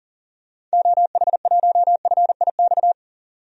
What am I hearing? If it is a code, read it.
Morse code, heard as a keyed tone, about 37 wpm, 700 Hz. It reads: OH1FIX
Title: DXCC-100 Call Signs - 6 Characters - 40wpm